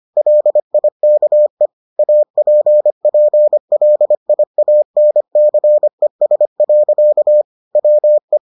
Morse code, heard as a keyed tone, about 25 wpm, 600 Hz. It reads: LIKE APPLIANCES. WE